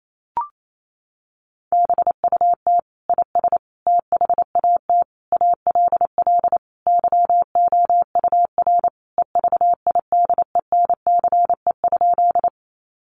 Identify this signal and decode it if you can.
Morse code, heard as a keyed tone, about 28 wpm, 700 Hz. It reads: BUT IS THAT ALL YOUR EVIDENCE?
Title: Sentences from Top 1300 Words 28wpm